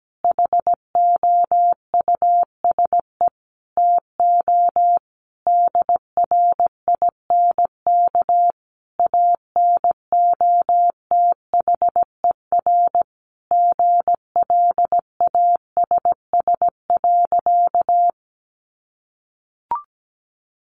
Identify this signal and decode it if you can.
Morse code, heard as a keyed tone, about 17 wpm, 700 Hz. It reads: HOUSE TO DRINK ANOTHER GLASS.
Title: Sentences from Top 1500 Words 17wpm